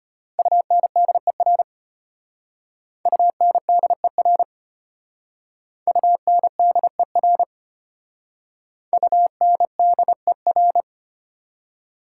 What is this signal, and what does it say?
Morse code, heard as a keyed tone, about 31 wpm, 700 Hz. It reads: UNDER UNDER UNDER UNDER
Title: Speed-Racing Prefixes and Suffixes 25wpm